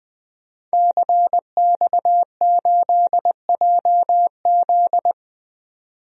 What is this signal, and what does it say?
Morse code, heard as a keyed tone, about 20 wpm, 700 Hz. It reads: CX8JZ
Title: DXCC-100 Call Signs - 5 Characters - 20wpm